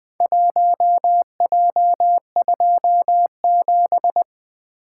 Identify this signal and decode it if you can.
Morse code, heard as a keyed tone, about 20 wpm, 700 Hz. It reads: 1J27
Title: Four Letter-Number Combinations - 20wpm